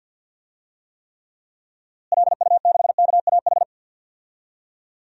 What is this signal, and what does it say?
Morse code, heard as a keyed tone, about 52 wpm, 700 Hz. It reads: LU6KAH
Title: DXCC-100 Call Signs - 6 Characters - Repeated 2x - 50wpm